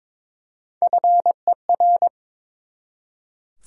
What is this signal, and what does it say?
Morse code, heard as a keyed tone, about 22 wpm, 700 Hz. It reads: FER
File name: Top 20 QSO Elements - 22wpm